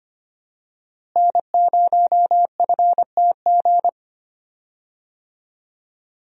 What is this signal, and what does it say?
Morse code, heard as a keyed tone, about 25 wpm, 700 Hz. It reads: N0FTG
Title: USA Call Signs - 5 Characters - Repeated 2x - 25wpm